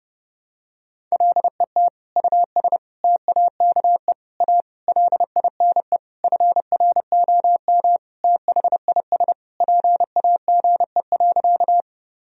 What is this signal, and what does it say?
Morse code, heard as a keyed tone, about 30 wpm, 700 Hz. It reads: LET US TAKE A LINE FROM THIS PAGE.
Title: Sentences from Top 1000 Words - Review Part 1 - 30wpm